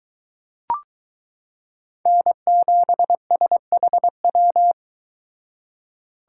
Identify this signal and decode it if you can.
Morse code, heard as a keyed tone, about 22 wpm, 700 Hz. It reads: N7SHW